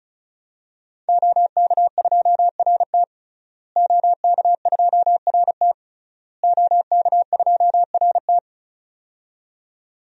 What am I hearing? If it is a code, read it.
Morse code, heard as a keyed tone, about 36 wpm, 700 Hz. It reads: OK2RT OK2RT OK2RT
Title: DXCC-100 Call Signs - 5 Characters - Repeated 3x - 35wpm